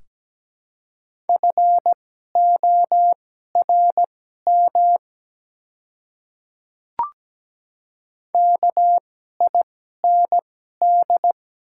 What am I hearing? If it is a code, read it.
Morse code, heard as a keyed tone, about 17 wpm, spaced as for 12 wpm, 700 Hz. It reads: FORM KIND
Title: 2x Character Spacing - Top 200 Words - 17wpm